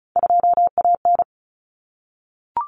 Morse code, heard as a keyed tone, about 34 wpm, 700 Hz. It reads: 2AN
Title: Common Cut Numbers - 3-Digit 35wpm